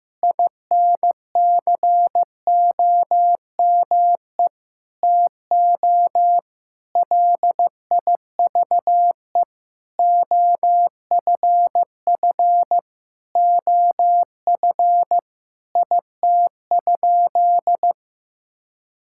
Morse code, heard as a keyed tone, about 15 wpm, 700 Hz. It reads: INCOME TO LIVE OFF OF IT?
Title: Sentences from Top 1000 Words - Review Part 5 - 15wpm